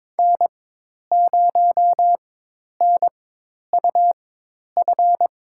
Morse code, heard as a keyed tone, about 22 wpm, spaced as for 10 wpm, 700 Hz. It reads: N0NUF
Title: USA Call Signs - 5 Characters - 4x Character Spacing - 22wpm